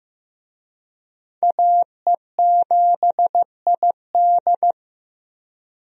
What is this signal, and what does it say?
Morse code, heard as a keyed tone, about 15 wpm, 700 Hz. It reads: AE7ID
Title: USA Call Signs - 5 Characters - Repeated 3x - 15wpm